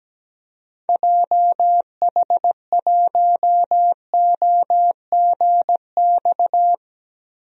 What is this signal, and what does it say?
Morse code, heard as a keyed tone, about 17 wpm, 700 Hz. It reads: JH1OGX
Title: DXCC-100 Call Signs - 6 Characters - Repeated 3x - 17wpm